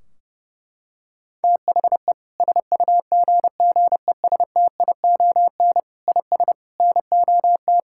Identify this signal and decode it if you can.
Morse code, heard as a keyed tone, about 30 wpm, 700 Hz. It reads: THE SUGGESTION IS NOT
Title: Sentences from Top 1600 Words 30wpm